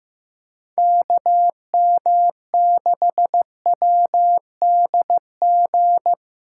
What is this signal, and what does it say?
Morse code, heard as a keyed tone, about 15 wpm, 700 Hz. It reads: KM6WDG
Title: USA Call Signs - 6 Characters - 8x Character Spacing - 15wpm